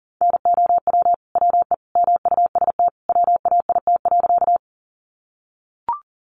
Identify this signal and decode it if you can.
Morse code, heard as a keyed tone, about 39 wpm, 700 Hz. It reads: NOW WE MUST WAIT.
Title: Sentences from Top 500 Words 40wpm